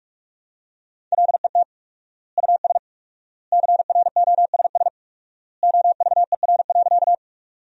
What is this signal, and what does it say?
Morse code, heard as a keyed tone, about 46 wpm, 700 Hz. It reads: LET US CROSS OVER.
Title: 4x Word-Spacing - Sentences from Top 300 Words - 45wpm